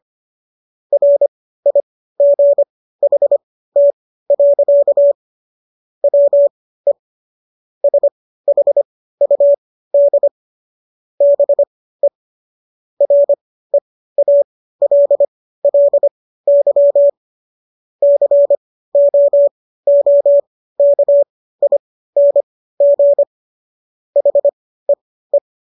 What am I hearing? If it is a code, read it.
Morse code, heard as a keyed tone, about 25 wpm, spaced as for 15 wpm, 570 Hz. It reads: RIGHT. WE SHUD BE REALLY COOKING HEE